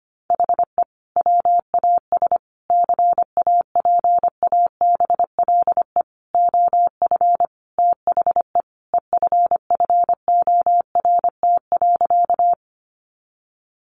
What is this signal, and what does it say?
Morse code, heard as a keyed tone, about 25 wpm, 700 Hz. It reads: HE WAS CAPABLE OF THE EFFORT.
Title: Sentences from Top 1200 Words 25wpm